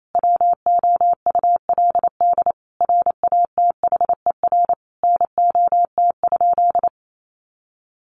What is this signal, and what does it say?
Morse code, heard as a keyed tone, about 28 wpm, 700 Hz. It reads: WOULD RATHER NOT?
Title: Sentences from Top 700 Words 28wpm